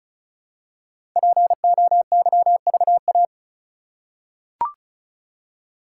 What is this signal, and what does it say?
Morse code, heard as a keyed tone, about 36 wpm, 700 Hz. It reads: POYVA
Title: Code Groups - 5 Letters - 35wpm